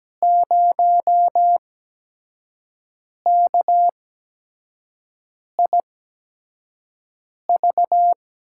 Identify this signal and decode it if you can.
Morse code, heard as a keyed tone, about 17 wpm, spaced as for 5 wpm, 700 Hz. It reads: 0KIV
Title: USA Call Signs - 8x Character Spacing - 17wpm